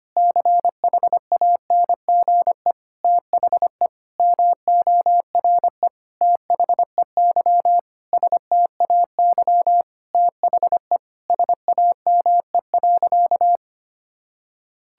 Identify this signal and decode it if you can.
Morse code, heard as a keyed tone, about 25 wpm, 700 Hz. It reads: CHANGE THE MORE THEY STAY THE SAME.
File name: practice / 25wpm / Sentences with 9-Word Phrases - 25wpm